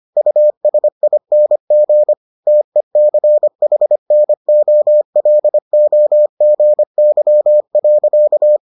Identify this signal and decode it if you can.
Morse code, heard as a keyed tone, about 25 wpm, 600 Hz. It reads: USING TECHNOLOGY.